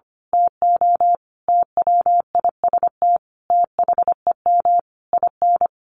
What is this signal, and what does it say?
Morse code, heard as a keyed tone, about 25 wpm, 700 Hz. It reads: TO TWIST THEM IN
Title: A Sound Of Thunder 25wpm